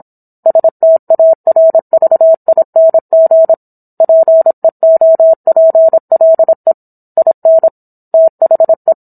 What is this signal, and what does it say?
Morse code, heard as a keyed tone, about 27 wpm, 650 Hz. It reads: STARVING PEOPLE IN THE